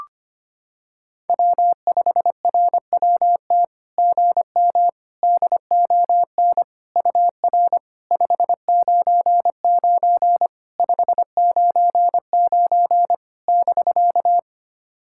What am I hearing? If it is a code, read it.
Morse code, heard as a keyed tone, about 25 wpm, 690 Hz. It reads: W5RWT GM DON UR 599 599 <BK>
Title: SOTA QSO Practice 25wpm